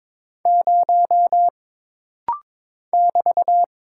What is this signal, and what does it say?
Morse code, heard as a keyed tone, about 22 wpm, 700 Hz. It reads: 0 <BT>
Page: This rapid-fire timing is geared to push your ICR (Instant Character Recognition) skills.